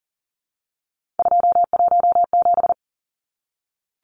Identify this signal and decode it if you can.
Morse code, heard as a keyed tone, about 40 wpm, 700 Hz. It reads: 217
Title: Numbers - 3-Digit 40wpm